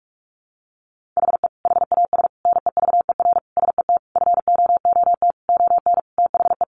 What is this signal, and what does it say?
Morse code, heard as a keyed tone, about 44 wpm, 700 Hz. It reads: HE HAS NEVER SET FOOT ON THE
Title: Sentences from Top 1500 Words 45wpm